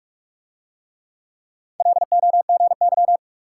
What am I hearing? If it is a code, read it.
Morse code, heard as a keyed tone, about 43 wpm, 700 Hz. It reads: LOGY